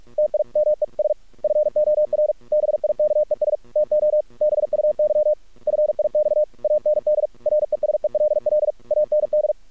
Morse code, heard as a keyed tone, about 46 wpm, 600 Hz. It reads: TIMES FOR BIKES TO BUY FIX AND RESELL AND
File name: archived / CRSnet